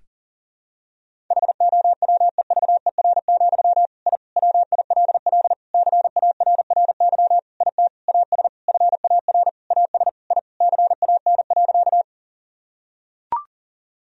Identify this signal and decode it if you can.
Morse code, heard as a keyed tone, about 40 wpm, 700 Hz. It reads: HOWEVER, I WILL CARRY IT AS FAR AS I CAN.
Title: Sentences from Top 1000 Words - Review Part 1 - 40wpm